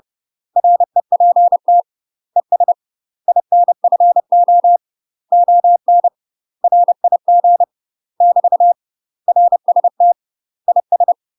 Morse code, heard as a keyed tone, about 30 wpm, 700 Hz. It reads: REPT ES INFO ON RIG <BT> RST IS
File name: UK MASTER QSOs TRAINING / 30WPM / UK MASTER QSOs 0002